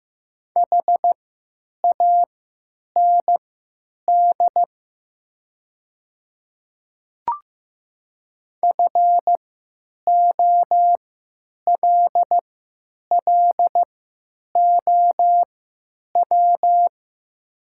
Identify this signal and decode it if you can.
Morse code, heard as a keyed tone, about 15 wpm, spaced as for 9 wpm, 700 Hz. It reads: HAND FOLLOW
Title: 3x Character Spacing - Top 200 Words - 15wpm